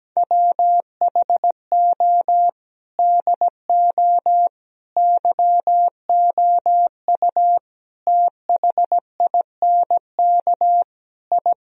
Morse code, heard as a keyed tone, about 17 wpm, 700 Hz. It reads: WHO DO YOU THINK I
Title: Sentences from Top 1000 Words - Review Part 4 - 17wpm